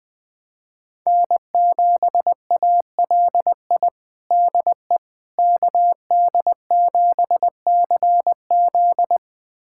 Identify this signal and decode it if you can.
Morse code, heard as a keyed tone, about 20 wpm, 700 Hz. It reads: N7ALI DE KD7CZ
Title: USA Callsign DE Callsign - 5 Characters - 20wpm